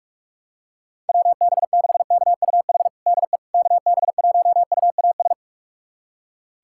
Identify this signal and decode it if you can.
Morse code, heard as a keyed tone, about 46 wpm, 700 Hz. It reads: WB6KUH DE KB1UAS
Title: USA Callsign DE Callsign - 6 Characters - 45wpm